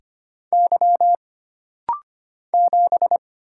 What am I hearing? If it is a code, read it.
Morse code, heard as a keyed tone, about 25 wpm, 700 Hz. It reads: Y 7
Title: Single Letter-Number - Rapid-Fire 25wpm